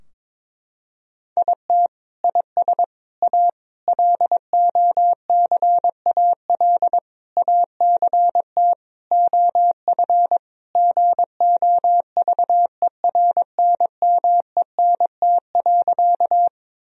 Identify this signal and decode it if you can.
Morse code, heard as a keyed tone, about 22 wpm, 700 Hz. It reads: IT IS A LOCAL ACT OF GOVERNMENT.
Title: Sentences from Top 1000 Words - Review Part 3 - 22wpm